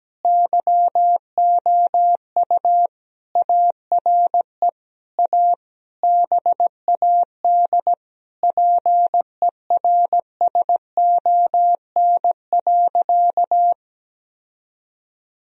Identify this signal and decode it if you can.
Morse code, heard as a keyed tone, about 17 wpm, 700 Hz. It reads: YOU ARE A BAD PERSON.
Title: Sentences from Top 1000 Words - Review Part 3 - 17wpm